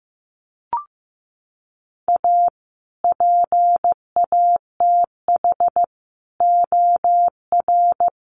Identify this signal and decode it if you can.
Morse code, heard as a keyed tone, about 15 wpm, 700 Hz. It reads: A PATH OR
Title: Sentences from Top 1400 Words 15wpm